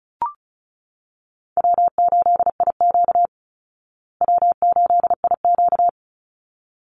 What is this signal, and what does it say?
Morse code, heard as a keyed tone, about 34 wpm, 700 Hz. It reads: W8IQ W8IQ
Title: Call Signs Repeated 3x - 35wpm